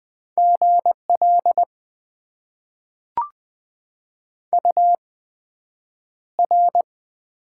Morse code, heard as a keyed tone, about 20 wpm, spaced as for 9 wpm, 700 Hz. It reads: GL UR
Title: Top 90 QSO Elements - 8x Character Spacing - 20wpm